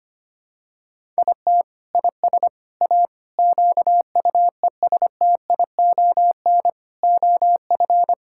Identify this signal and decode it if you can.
Morse code, heard as a keyed tone, about 25 wpm, 700 Hz. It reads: IT IS A QUESTION OF
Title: Sentences from Top 1000 Words - Review Part 6 - 25wpm